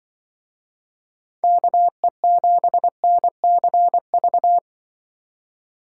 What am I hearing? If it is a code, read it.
Morse code, heard as a keyed tone, about 26 wpm, 700 Hz. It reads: KE7NCV